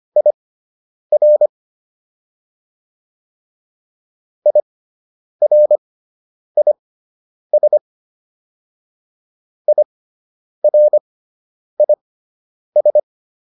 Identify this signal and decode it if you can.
Morse code, heard as a keyed tone, about 25 wpm, spaced as for 9 wpm, 600 Hz. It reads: IR IRIS IRIS